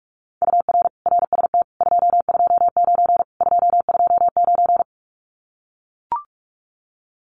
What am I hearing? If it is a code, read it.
Morse code, heard as a keyed tone, about 44 wpm, 700 Hz. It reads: UR RST 229 229
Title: Common QSO Phrases Encore 45wpm